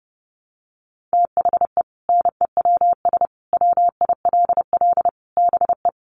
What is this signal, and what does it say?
Morse code, heard as a keyed tone, about 30 wpm, 700 Hz. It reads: THE NEWS WILL BE
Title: Sentences from Top 1200 Words 30wpm